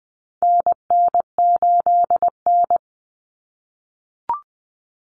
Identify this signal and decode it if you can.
Morse code, heard as a keyed tone, about 19 wpm, 700 Hz. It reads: NN8N
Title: Common Cut Numbers - 4-Digit 20wpm